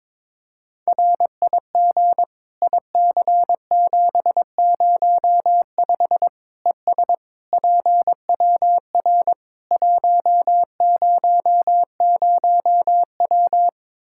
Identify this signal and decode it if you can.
Morse code, heard as a keyed tone, about 23 wpm, 700 Hz. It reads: RIG IC705 ES PWR 100W
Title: Practice QSO - Exchange 2 - Info - 22wpm